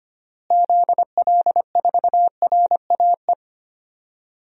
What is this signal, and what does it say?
Morse code, heard as a keyed tone, about 25 wpm, 700 Hz. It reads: ZL4RAE
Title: DXCC-100 Call Signs - 6 Characters - Repeated 3x - 25wpm